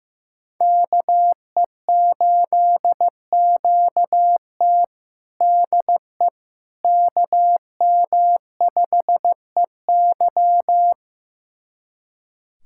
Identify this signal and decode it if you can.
Morse code, heard as a keyed tone, about 15 wpm, 700 Hz. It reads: KE8QT DE KM5EY
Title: USA Callsign DE Callsign - 5 Characters - 15wpm